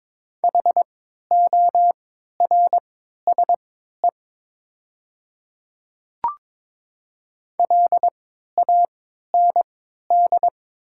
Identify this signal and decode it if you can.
Morse code, heard as a keyed tone, about 22 wpm, spaced as for 12 wpm, 700 Hz. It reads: HORSE LAND
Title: 3x Character Spacing - Top 300 Words - 22wpm